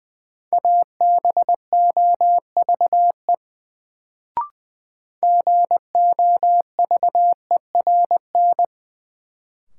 Morse code, heard as a keyed tone, about 20 wpm, 700 Hz. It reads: ABOVE GOVERN
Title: Sentences from Top 1000 Words - Review Part 2 - New Word Drill - 20wpm